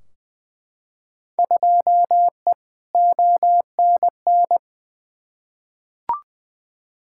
Morse code, heard as a keyed tone, about 20 wpm, 700 Hz. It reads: 2E ONN
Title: Field Day Exchanges Encore - 20wpm